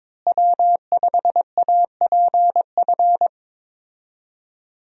Morse code, heard as a keyed tone, about 22 wpm, 700 Hz. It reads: W5APF
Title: USA Call Signs - 5 Characters - Repeated 3x - 22wpm